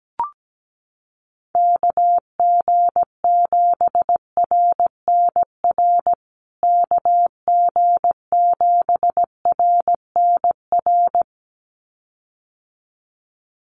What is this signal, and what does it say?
Morse code, heard as a keyed tone, about 17 wpm, 700 Hz. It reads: KG7RNR KG7RNR
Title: USA Call Signs Repeated 2x 17wpm